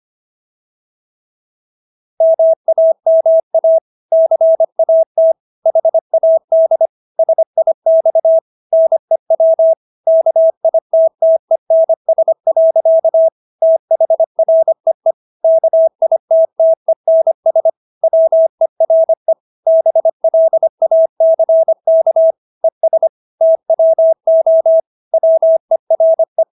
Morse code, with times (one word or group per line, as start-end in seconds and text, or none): 2.2-3.8 MAMA
4.1-5.3 CAT
5.7-6.9 HAD
7.2-8.4 SIX
8.7-9.7 NEW
10.1-13.3 KITTENS.
13.6-15.1 THREE
15.4-17.7 KITTENS
18.0-19.3 WERE
19.7-22.3 BLACK
22.6-23.1 ES
23.4-24.8 TWO
25.1-26.4 WERE